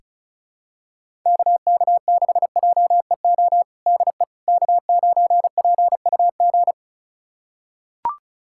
Morse code, heard as a keyed tone, about 36 wpm, 700 Hz. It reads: KK6JEO DE K9PUG
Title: USA Callsign DE Callsign 35wpm